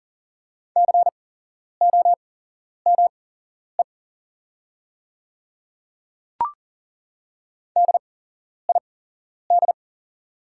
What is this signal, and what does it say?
Morse code, heard as a keyed tone, about 40 wpm, spaced as for 11 wpm, 690 Hz. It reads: COME DID
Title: 8x Character Spacing - Top 100 Words - 40wpm